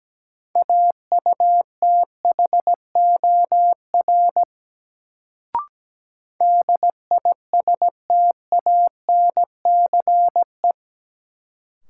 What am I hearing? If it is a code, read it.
Morse code, heard as a keyed tone, about 17 wpm, 700 Hz. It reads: AUTHOR DISTANCE
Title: Sentences from Top 1200 Words - New Word Drill - 17wpm